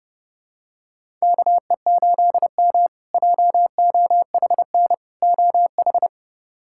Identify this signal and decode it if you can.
Morse code, heard as a keyed tone, about 30 wpm, 700 Hz. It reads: KE8M JOHN OH
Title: K1USN SST QSO Practice - 30wpm